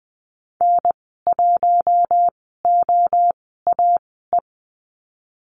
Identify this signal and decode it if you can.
Morse code, heard as a keyed tone, about 20 wpm, spaced as for 14 wpm, 700 Hz. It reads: N1OAE
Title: 055 Lesson 10g - Letter r - Call Signs 20wpm